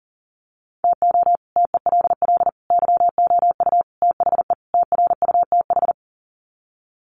Morse code, heard as a keyed tone, about 40 wpm, 700 Hz. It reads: TO TELL YOU THE TRUTH
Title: Sets of 5 Words 40wpm